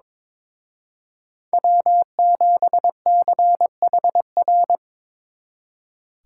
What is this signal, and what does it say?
Morse code, heard as a keyed tone, about 26 wpm, 700 Hz. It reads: W7CHR
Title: Speed Racing - USA 1x3 Call Signs 22wpm